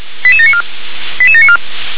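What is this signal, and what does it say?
Five-tone selective call (ZVEI), heard as quick runs of five tones, tones of about 70 ms. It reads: 88794 88794
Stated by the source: ZVEI five-tone selective call received by SDR.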